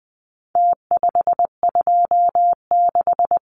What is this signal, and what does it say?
Morse code, heard as a keyed tone, about 20 wpm, 700 Hz. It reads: T526
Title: Numbers - Contest Style - 4-Digit 20wpm